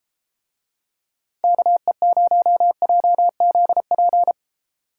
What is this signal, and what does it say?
Morse code, heard as a keyed tone, about 33 wpm, 700 Hz. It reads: KE0JZP